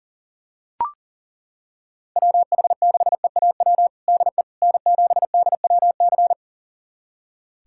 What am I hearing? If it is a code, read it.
Morse code, heard as a keyed tone, about 40 wpm, 700 Hz. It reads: WH6EAW DE N7DWC
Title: USA Callsign DE Callsign 40wpm